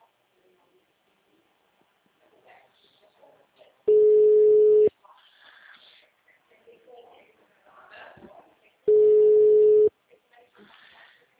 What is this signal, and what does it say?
Ringback tone (Europe, 425 Hz), with speech.
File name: out- -2045- - - .44301